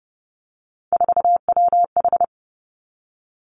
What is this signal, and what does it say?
Morse code, heard as a keyed tone, about 30 wpm, 700 Hz. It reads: 4WH
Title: Three Challenging Letter-Number Combinations 30wpm